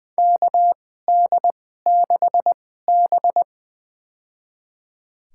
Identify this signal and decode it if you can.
Morse code, heard as a keyed tone, about 20 wpm, spaced as for 14 wpm, 700 Hz. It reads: KD6B
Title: USA Call Signs - 4 Characters - 2x Character Spacing - 20wpm